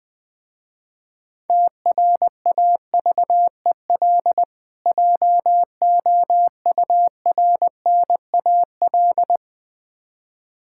Morse code, heard as a keyed tone, about 21 wpm, 700 Hz. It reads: TRAVEL JOURNAL
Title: Difficult Words with Hints - 20wpm